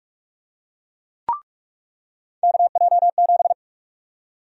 Morse code, heard as a keyed tone, about 45 wpm, 700 Hz. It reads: KJ7
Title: USA Call Sign Prefixes - 45wpm